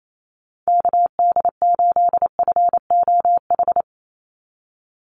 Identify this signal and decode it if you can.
Morse code, heard as a keyed tone, about 28 wpm, 700 Hz. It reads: KD8FOH
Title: USA Call Signs 28wpm